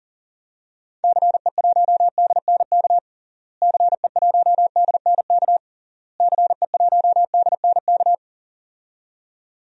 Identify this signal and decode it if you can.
Morse code, heard as a keyed tone, about 37 wpm, 700 Hz. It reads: CE1DNK CE1DNK CE1DNK
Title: DXCC-100 Call Signs - 6 Characters - Repeated 3x - 40wpm